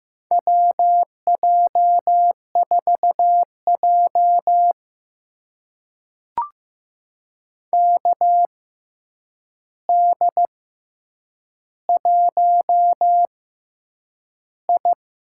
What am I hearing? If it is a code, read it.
Morse code, heard as a keyed tone, about 15 wpm, spaced as for 8 wpm, 700 Hz. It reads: WJ4J KD1I
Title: USA Call Signs - 4 Characters - 6x Character Spacing - 15wpm